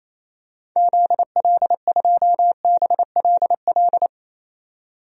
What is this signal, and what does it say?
Morse code, heard as a keyed tone, about 28 wpm, 700 Hz. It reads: ZL2BLL
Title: DXCC-100 Call Signs - 6 Characters - 28wpm